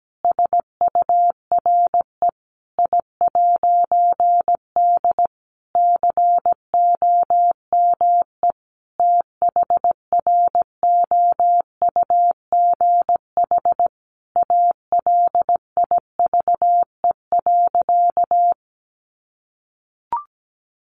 Morse code, heard as a keyed tone, about 17 wpm, 700 Hz. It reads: SURE I'D COME THROUGH ALIVE.
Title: A Sound Of Thunder 17wpm